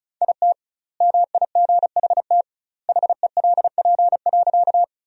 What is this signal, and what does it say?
Morse code, heard as a keyed tone, about 34 wpm, 700 Hz. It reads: IT MIGHT HELP.
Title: 2x Word-Spacing - Sentences from Top 300 Words - 35wpm